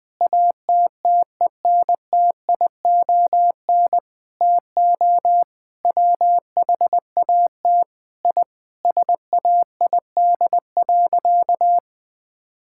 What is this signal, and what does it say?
Morse code, heard as a keyed tone, about 20 wpm, 700 Hz. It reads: ATTENTION TO WHAT I SAID.
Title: Sentences from Top 1000 Words - Review Part 6 - 20wpm